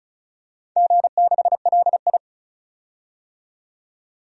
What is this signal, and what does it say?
Morse code, heard as a keyed tone, about 34 wpm, 700 Hz. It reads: G6LI
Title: DXCC-100 Call Signs - 4 Characters - 35wpm